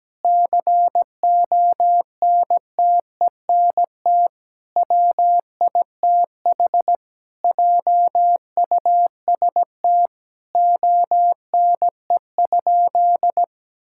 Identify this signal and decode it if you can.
Morse code, heard as a keyed tone, about 17 wpm, 700 Hz. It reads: CONTENT WITH JUST ONE?
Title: Sentences from Top 1000 Words - Review Part 6 - 17wpm